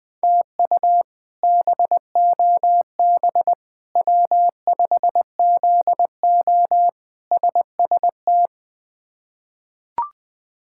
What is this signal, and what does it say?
Morse code, heard as a keyed tone, about 19 wpm, 700 Hz. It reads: TU BOB W5ZO SST
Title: K1USN SST QSO Practice - 20wpm